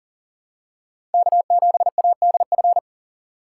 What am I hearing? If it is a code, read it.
Morse code, heard as a keyed tone, about 40 wpm, 700 Hz. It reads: K7ADF
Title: USA Call Signs - 5 Characters - 2x Character Spacing - 40wpm